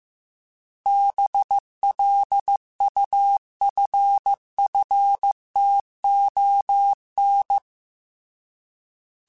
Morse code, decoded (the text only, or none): BLUFFTON